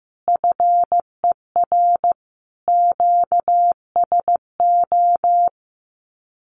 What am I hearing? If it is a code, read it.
Morse code, heard as a keyed tone, about 15 wpm, 700 Hz. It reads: FER QSO
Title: Common QSO Phrases 15wpm